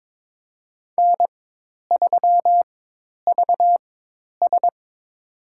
Morse code, heard as a keyed tone, about 22 wpm, spaced as for 10 wpm, 700 Hz. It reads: N3VS
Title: USA Call Signs - 4 Characters - 4x Character Spacing - 22wpm